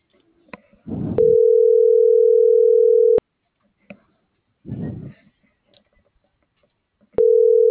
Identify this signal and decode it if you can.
Ringback tone (North America), 440 + 480 Hz.